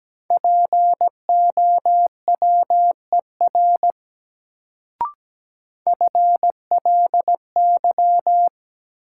Morse code, heard as a keyed tone, about 17 wpm, 700 Hz. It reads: POWER FLY
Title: Sentences from Top 500 Words - New Word Drill - 17wpm